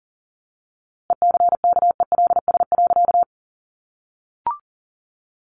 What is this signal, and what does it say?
Morse code, heard as a keyed tone, about 40 wpm, 700 Hz. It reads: ECKELS.
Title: A Sound Of Thunder 40wpm